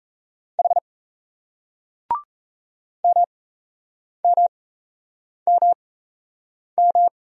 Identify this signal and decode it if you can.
Morse code, heard as a keyed tone, about 38 wpm, spaced as for 8 wpm, 700 Hz. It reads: H MMMM